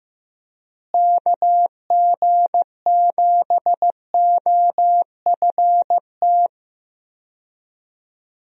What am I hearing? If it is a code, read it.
Morse code, heard as a keyed tone, about 15 wpm, 700 Hz. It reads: KG7OFT
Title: USA Call Signs - 6 Characters - Repeated 3x - 15wpm